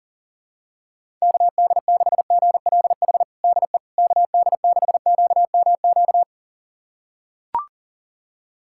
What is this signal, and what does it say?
Morse code, heard as a keyed tone, about 37 wpm, 700 Hz. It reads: KD6GLH DE KD6QMQ
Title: USA Callsign DE Callsign - 6 Characters - 40wpm